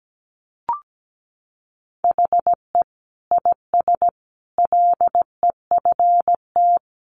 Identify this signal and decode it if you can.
Morse code, heard as a keyed tone, about 17 wpm, 700 Hz. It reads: HE IS LEFT
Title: Sentences from Top 1100 Words 17wpm